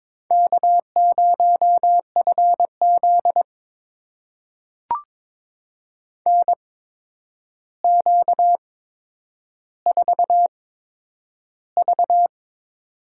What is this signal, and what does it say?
Morse code, heard as a keyed tone, about 22 wpm, spaced as for 9 wpm, 700 Hz. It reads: K0FZ NQ4V
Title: USA Call Signs - 4 Characters - 8x Character Spacing - 22wpm